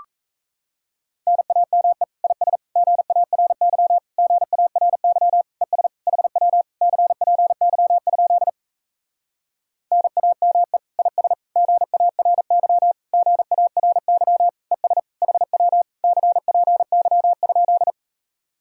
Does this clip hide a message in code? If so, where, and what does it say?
Yes: Morse code from 1.3 to 17.9 s, heard as a keyed tone, about 37 wpm, 700 Hz. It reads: NAME IS GARY GARY ES HW CPY? NAME IS GARY GARY ES HW CPY?